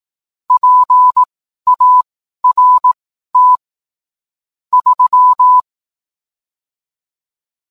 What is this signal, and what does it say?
Morse code, heard as a keyed tone, about 18 wpm, spaced as for 12 wpm, 1000 Hz. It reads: PART 3